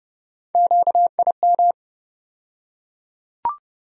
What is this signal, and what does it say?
Morse code, heard as a keyed tone, about 30 wpm, 700 Hz. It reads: QIM
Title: USA Call Sign Postfixes - 3 Letters 30wpm